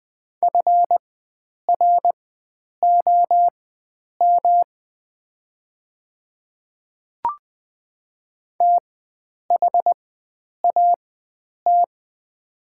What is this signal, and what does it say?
Morse code, heard as a keyed tone, about 20 wpm, spaced as for 9 wpm, 700 Hz. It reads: FROM THAT
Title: 4x Character Spacing - Top 100 Words - 20wpm